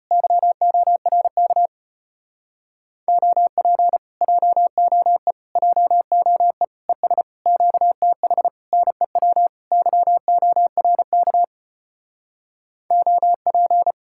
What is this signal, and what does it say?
Morse code, heard as a keyed tone, about 33 wpm, 700 Hz. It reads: YORK OP JOE JOE ES QTH NEW YORK OP